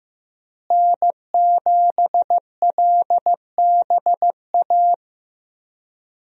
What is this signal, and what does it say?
Morse code, heard as a keyed tone, about 14 wpm, 700 Hz. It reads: N7LBA
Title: USA Call Signs - 5 Characters - Repeated 2x - 15wpm